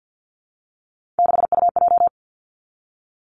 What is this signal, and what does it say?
Morse code, heard as a keyed tone, about 50 wpm, 700 Hz. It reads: 6UJ
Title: Three Challenging Letter-Number Combinations 50wpm